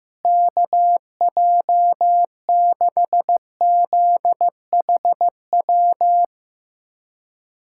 Morse code, heard as a keyed tone, about 16 wpm, 700 Hz. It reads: KJ6ZHW
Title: USA Callsign DE Callsign - 6 Characters - 15wpm